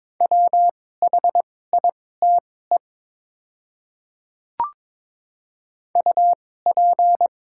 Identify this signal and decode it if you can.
Morse code, heard as a keyed tone, about 22 wpm, spaced as for 16 wpm, 700 Hz. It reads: WHITE UP
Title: 2x Character Spacing - Top 300 Words - 22wpm